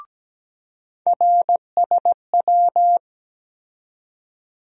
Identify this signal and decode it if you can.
Morse code, heard as a keyed tone, about 17 wpm, 700 Hz. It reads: RSW
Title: USA Call Sign Postfixes - 3 Letters 17wpm